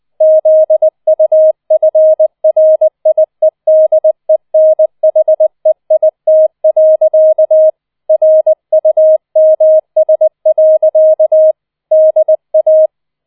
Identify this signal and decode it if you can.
Morse code, heard as a keyed tone, about 20 wpm, 630 Hz. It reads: ZUFRIEDENHEIT. RUMS. DA